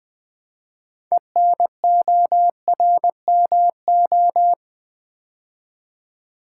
Morse code, heard as a keyed tone, about 21 wpm, 700 Hz. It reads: ENORMO
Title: Build A Word - 3 Letters At A Time - 8-Letter Words - 20wpm